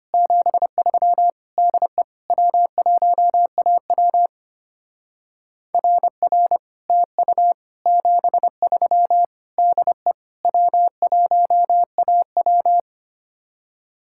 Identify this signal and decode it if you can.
Morse code, heard as a keyed tone, about 31 wpm, 700 Hz. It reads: 73 DE W1AW RR TU 73 DE W1AW
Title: Speed-Racing Common QSO Phrases Encore - 25wpm